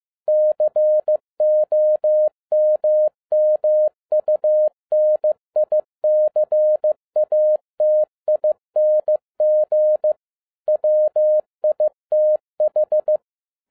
Morse code, heard as a keyed tone, about 15 wpm, 610 Hz. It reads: COMMUNICATING WITH